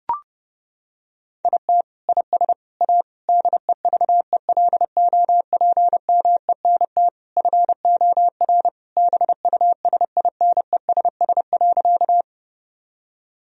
Morse code, heard as a keyed tone, about 30 wpm, 700 Hz. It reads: IT IS A DEVELOPMENT FOR BUSINESS.
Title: Sentences from Top 1000 Words - Review Part 4 - 30wpm